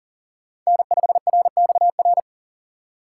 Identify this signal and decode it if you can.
Morse code, heard as a keyed tone, about 40 wpm, 700 Hz. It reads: N5RXR
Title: USA Call Signs - 5 Characters - 6x Character Spacing - 40wpm